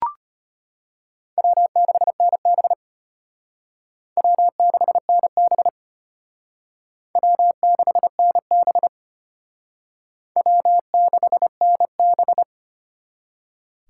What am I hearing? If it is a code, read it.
Morse code, heard as a keyed tone, about 32 wpm, 700 Hz. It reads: W6NB W6NB W6NB W6NB